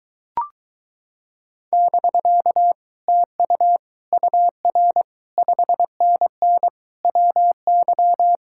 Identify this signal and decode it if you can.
Morse code, heard as a keyed tone, about 23 wpm, 700 Hz. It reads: <BK> TU UR 5NN WY